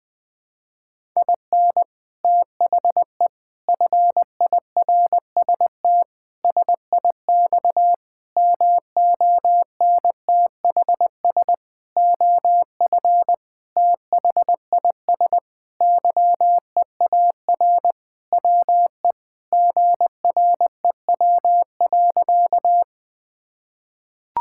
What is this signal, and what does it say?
Morse code, heard as a keyed tone, about 20 wpm, 700 Hz. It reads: IN THE FIRST SIX MONTHS OF THIS YEAR WE GREW.
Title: Sentences with 8-Word Phrases - Spoken Beginning - 20wpm